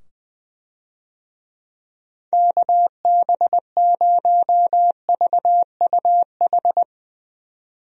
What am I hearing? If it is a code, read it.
Morse code, heard as a keyed tone, about 20 wpm, 700 Hz. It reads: KB0VUH
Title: USA Call Signs - 6 Characters - 20wpm